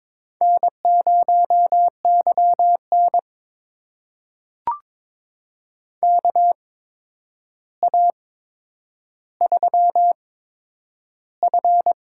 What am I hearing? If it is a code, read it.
Morse code, heard as a keyed tone, about 22 wpm, spaced as for 9 wpm, 700 Hz. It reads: N0YN KA3F
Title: USA Call Signs - 8x Character Spacing - 22wpm